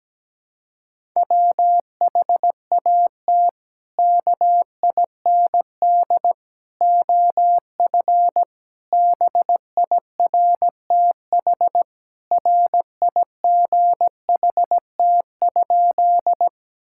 Morse code, heard as a keyed tone, about 17 wpm, 700 Hz. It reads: WHAT KIND OF BIRTHRIGHT?
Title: Sentences from Top 1000 Words - Review Part 5 - 17wpm